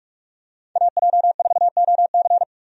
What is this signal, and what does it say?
Morse code, heard as a keyed tone, about 46 wpm, 700 Hz. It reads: AJ4OC
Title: USA Call Signs - 5 Characters - 45wpm